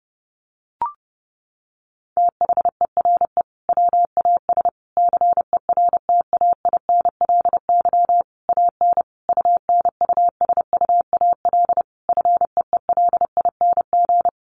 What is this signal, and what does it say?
Morse code, heard as a keyed tone, about 30 wpm, 700 Hz. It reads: THERE WAS CERTAINLY AN UNUSUAL FEELING
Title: Sentences from Top 1400 Words 30wpm